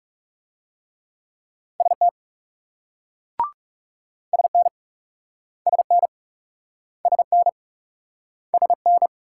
Morse code, heard as a keyed tone, about 40 wpm, 700 Hz. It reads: ST SN SN SN SN